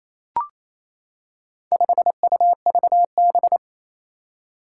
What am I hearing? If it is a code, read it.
Morse code, heard as a keyed tone, about 27 wpm, 700 Hz. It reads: 5UVB
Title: Four Challenging Letter-Number Combinations 28wpm